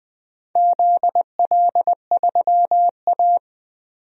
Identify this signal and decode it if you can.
Morse code, heard as a keyed tone, about 20 wpm, 700 Hz. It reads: ZL3A
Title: DXCC-100 Call Signs - 4 Characters - 20wpm